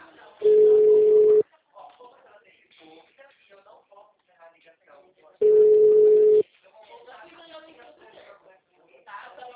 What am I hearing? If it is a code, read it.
Ringback tone (Europe, 425 Hz), with speech.
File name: out- -2032- - - .9247